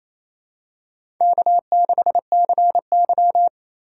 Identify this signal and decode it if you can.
Morse code, heard as a keyed tone, about 28 wpm, 700 Hz. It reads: K6CY
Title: USA Call Signs - 4 Characters - 3x Character Spacing - 28wpm